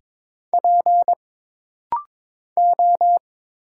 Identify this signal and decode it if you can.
Morse code, heard as a keyed tone, about 22 wpm, 700 Hz. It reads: P O
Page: This rapid-fire timing is geared to push your ICR (Instant Character Recognition) skills.